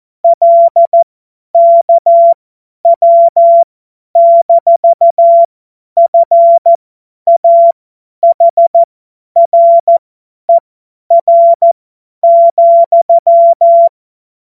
Morse code, heard as a keyed tone, about 14 wpm, spaced as for 10 wpm, 680 Hz. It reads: LKW-FAHRER,
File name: Morsekurs / Witze / Witz 01. Februar 2022 50 F70